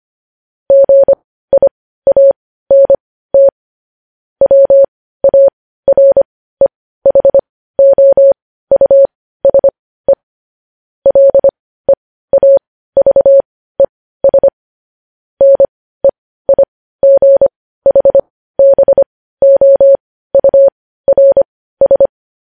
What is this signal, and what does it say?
Morse code, heard as a keyed tone, about 25 wpm, spaced as for 15 wpm, 550 Hz. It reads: GIANT WAREHOUSE LEAVES NEIGHBOURS
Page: news-headlines-in-morse-code-at-15-wpm